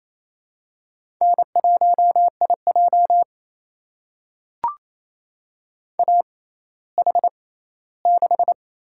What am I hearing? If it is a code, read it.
Morse code, heard as a keyed tone, about 28 wpm, spaced as for 16 wpm, 700 Hz. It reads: N1IJ AH6
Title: USA Call Signs - 4 Characters - 6x Character Spacing - 28wpm